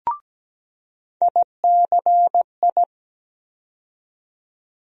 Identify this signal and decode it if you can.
Morse code, heard as a keyed tone, about 17 wpm, 700 Hz. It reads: ICI
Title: USA Call Sign Postfixes - 3 Letters 17wpm